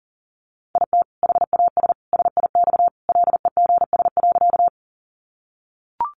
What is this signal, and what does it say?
Morse code, heard as a keyed tone, about 37 wpm, 700 Hz. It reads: IT HAS SIX LEGS.
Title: Additional Easy Phrases 40wpm